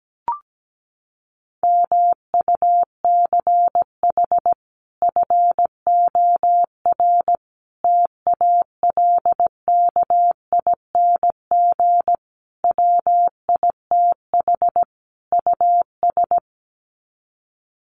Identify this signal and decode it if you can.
Morse code, heard as a keyed tone, about 17 wpm, 700 Hz. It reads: MUCH FOR TALKING WITH US
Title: Sets of 5 Words Encore 17wpm